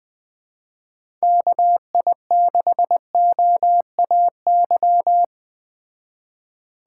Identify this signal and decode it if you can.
Morse code, heard as a keyed tone, about 20 wpm, 700 Hz. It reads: KI6OAY
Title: Speed Racing - USA 2x3 Call Signs 17wpm